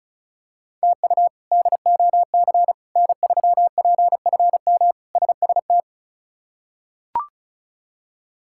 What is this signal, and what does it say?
Morse code, heard as a keyed tone, about 36 wpm, 700 Hz. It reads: TU DOC N3PFM SST
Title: K1USN SST QSO Practice - 35wpm